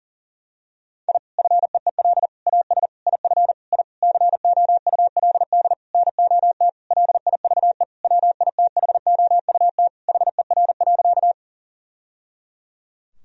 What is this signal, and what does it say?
Morse code, heard as a keyed tone, about 40 wpm, 700 Hz. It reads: I FEEL AS IF I COULD NOT LIVE WITHOUT HER.
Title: Sentences from Top 1000 Words - Review Part 3 - 40wpm